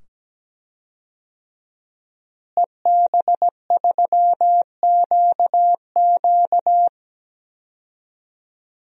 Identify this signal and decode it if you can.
Morse code, heard as a keyed tone, about 17 wpm, 700 Hz. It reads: EB3QQ
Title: DXCC-100 Call Signs - 5 Characters - Repeated 2x - 17wpm